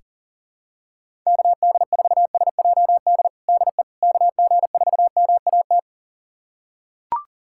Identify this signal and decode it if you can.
Morse code, heard as a keyed tone, about 40 wpm, 700 Hz. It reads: KD4SJD DE KG4MAT
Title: USA Callsign DE Callsign 40wpm